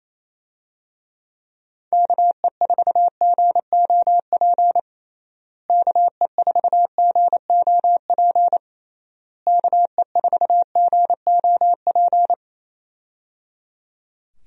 Morse code, heard as a keyed tone, about 28 wpm, 700 Hz. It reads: KE4GOP KE4GOP KE4GOP
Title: USA Call Signs - 6 Characters - Repeated 3x - 28wpm